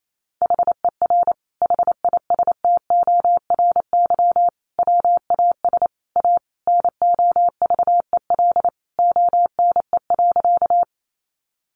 Morse code, heard as a keyed tone, about 28 wpm, 700 Hz. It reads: HER HISTORY WAS A NOVEL ONE.
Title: Sentences from Top 1300 Words 28wpm